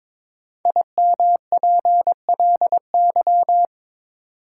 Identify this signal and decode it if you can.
Morse code, heard as a keyed tone, about 22 wpm, 700 Hz. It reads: IMPLY
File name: 5 Letter Words - First 3 Letters Given - 22wpm